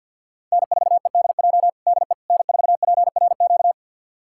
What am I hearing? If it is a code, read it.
Morse code, heard as a keyed tone, about 46 wpm, 700 Hz. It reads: N4EDJ DE N4PRQ
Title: USA Callsign DE Callsign - 5 Characters - 50wpm